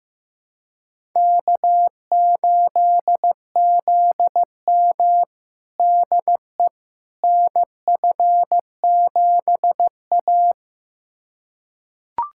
Morse code, heard as a keyed tone, about 15 wpm, 700 Hz. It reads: K8ZM DE NF7A
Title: USA Callsign DE Callsign - 4 Characters - 15wpm